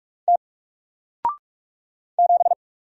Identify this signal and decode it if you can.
Morse code, heard as a keyed tone, about 44 wpm, 700 Hz. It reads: T 7
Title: Single Letter-Number Alt 45wpm